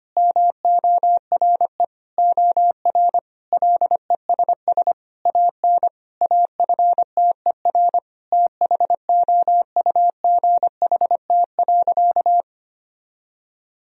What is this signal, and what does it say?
Morse code, heard as a keyed tone, about 25 wpm, 700 Hz. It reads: MORE OR LESS AN AFTER THOUGHT.
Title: Sentences from Top 1000 Words - Review Part 2 - 25wpm